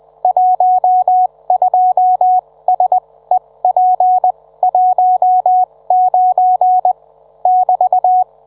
Morse code, heard as a keyed tone, about 20 wpm, spaced as for 17 wpm, 740 Hz. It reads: 12SEP19 <BT>